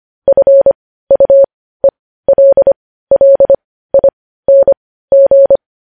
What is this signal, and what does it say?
Morse code, heard as a keyed tone, about 25 wpm, spaced as for 15 wpm, 550 Hz. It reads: FUELLING
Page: news headlines in morse code at 15 wpm